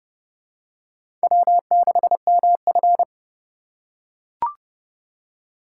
Morse code, heard as a keyed tone, about 32 wpm, 700 Hz. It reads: W6MF
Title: USA Call Signs - 4 Characters - 2x Character Spacing - 30wpm